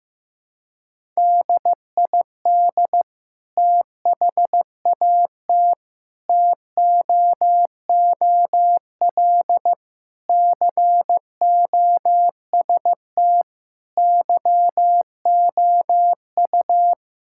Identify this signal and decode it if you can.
Morse code, heard as a keyed tone, about 15 wpm, 700 Hz. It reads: DID THAT TOOL COST YOU
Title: Sentences from Top 1000 Words - Review Part 6 - 15wpm